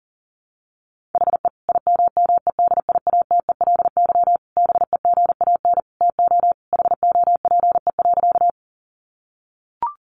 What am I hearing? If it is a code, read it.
Morse code, heard as a keyed tone, about 40 wpm, 700 Hz. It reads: HE IMMEDIATELY BEGAN TO HOPE.
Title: Sentences from Top 1100 Words 40wpm